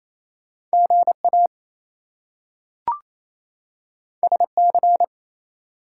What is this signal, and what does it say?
Morse code, heard as a keyed tone, about 28 wpm, 700 Hz. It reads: GA SC
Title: US States and Canadian Provinces 28wpm